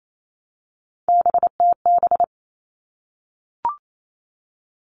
Morse code, heard as a keyed tone, about 28 wpm, 700 Hz. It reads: BTB
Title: All Cut Numbers - 3-Digit 28wpm